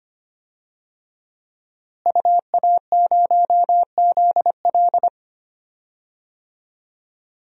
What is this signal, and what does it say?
Morse code, heard as a keyed tone, about 25 wpm, 700 Hz. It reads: UA0ZL
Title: DXCC-100 Call Signs - 5 Characters - Repeated 3x - 25wpm